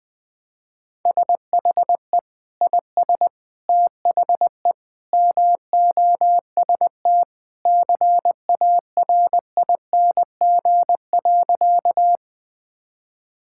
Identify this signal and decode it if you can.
Morse code, heard as a keyed tone, about 21 wpm, 700 Hz. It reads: SHE IS THE MOST CARING.
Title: Sentences with 2-Word Phrases - 20wpm